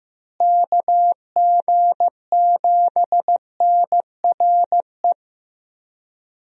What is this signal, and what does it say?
Morse code, heard as a keyed tone, about 15 wpm, 700 Hz. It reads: KG7NRE
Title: USA Call Signs - 6 Characters - Repeated 3x - 15wpm